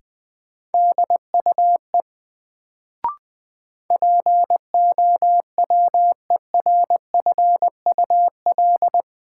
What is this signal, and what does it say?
Morse code, heard as a keyed tone, about 20 wpm, 700 Hz. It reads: DUE POWERFUL
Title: Sets of 5 Words Encore - New Word Drill - 20wpm